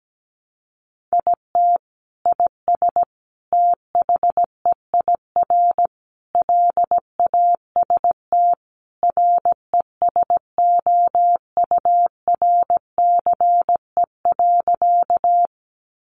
Morse code, heard as a keyed tone, about 17 wpm, 700 Hz. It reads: IT IS THEIR LAST RESOURCE.
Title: Sentences from Top 1300 Words 17wpm